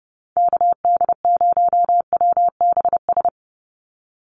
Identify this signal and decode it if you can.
Morse code, heard as a keyed tone, about 30 wpm, 700 Hz. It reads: KD0WBS
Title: USA Call Signs 30wpm